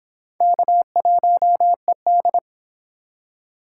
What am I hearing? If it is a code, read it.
Morse code, heard as a keyed tone, about 26 wpm, 700 Hz. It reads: K1ED